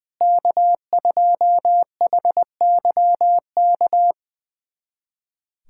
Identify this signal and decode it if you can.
Morse code, heard as a keyed tone, about 20 wpm, 700 Hz. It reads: K2HYK
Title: USA Callsign DE Callsign 20wpm